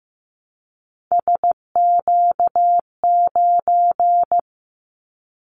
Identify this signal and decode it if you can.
Morse code, heard as a keyed tone, about 15 wpm, 700 Hz. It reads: SQ9
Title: Callsign Prefixes for Top 25 DXCC - Repeated 3x - 15wpm